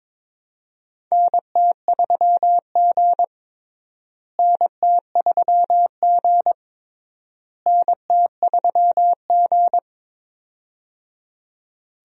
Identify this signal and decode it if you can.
Morse code, heard as a keyed tone, about 22 wpm, 700 Hz. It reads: NT3G NT3G NT3G
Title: USA Call Signs - 4 Characters - Repeated 3x - 22wpm